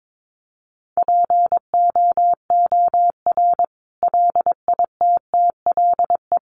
Morse code, heard as a keyed tone, about 22 wpm, 700 Hz. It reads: POOR LITTLE
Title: Sentences from Top 1000 Words 22wpm